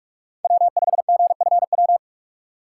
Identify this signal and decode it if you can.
Morse code, heard as a keyed tone, about 44 wpm, 700 Hz. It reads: W5GFW
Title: USA Call Signs - 5 Characters - 6x Character Spacing - 45wpm